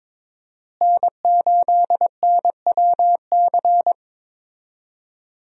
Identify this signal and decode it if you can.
Morse code, heard as a keyed tone, about 22 wpm, 700 Hz. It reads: N8NWC
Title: USA Call Signs - 5 Characters - Repeated 3x - 22wpm